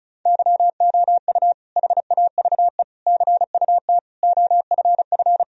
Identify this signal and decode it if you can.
Morse code, heard as a keyed tone, about 36 wpm, 700 Hz. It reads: YOU HAVE CUT OFF
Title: Sentences from Top 1000 Words - Review Part 1 - 35wpm